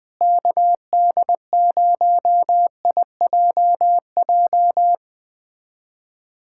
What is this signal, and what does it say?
Morse code, heard as a keyed tone, about 20 wpm, 700 Hz. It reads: KD0IJJ
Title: USA Callsign DE Callsign - 6 Characters - 20wpm